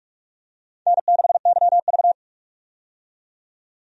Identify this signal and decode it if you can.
Morse code, heard as a keyed tone, about 47 wpm, 700 Hz. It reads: N6YV
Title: USA Call Signs - 4 Characters - 45wpm